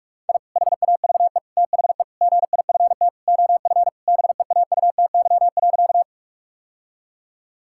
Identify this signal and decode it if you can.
Morse code, heard as a keyed tone, about 46 wpm, 700 Hz. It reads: I HAVE THE GIFT OF BEAUTY.
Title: Sentences from Top 1000 Words - Review Part 6 - 45wpm